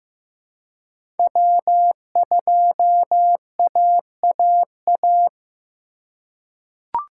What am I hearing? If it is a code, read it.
Morse code, heard as a keyed tone, about 15 wpm, 700 Hz. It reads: W2AAA
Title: USA Call Signs - 3x Character Spacing - 15wpm